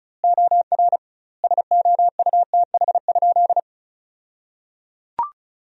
Morse code, heard as a keyed tone, about 36 wpm, 700 Hz. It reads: OR SOUTH?
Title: 2x Word-Spacing - Sentences from Top 400 Words - 35wpm